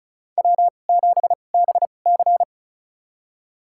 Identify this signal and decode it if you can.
Morse code, heard as a keyed tone, about 36 wpm, spaced as for 25 wpm, 700 Hz. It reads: W7BC
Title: USA Call Signs - 4 Characters - 2x Character Spacing - 35wpm